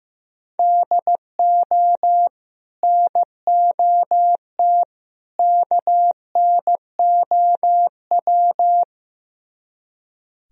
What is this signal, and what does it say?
Morse code, heard as a keyed tone, about 16 wpm, 700 Hz. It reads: DO NOT KNOW
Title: Speed-Racing - Sets of 3 Words 15wpm